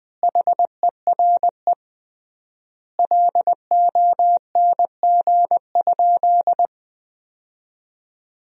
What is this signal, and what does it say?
Morse code, heard as a keyed tone, about 20 wpm, 700 Hz. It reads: HERE LONG?
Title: 3x Word-Spacing - Sentences from Top 200 Words - 20wpm